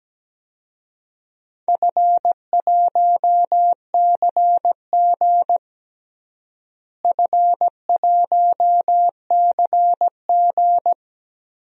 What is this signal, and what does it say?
Morse code, heard as a keyed tone, about 17 wpm, 700 Hz. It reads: F1CG F1CG
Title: DXCC-100 Call Signs - 4 Characters - Repeated 2x - 17wpm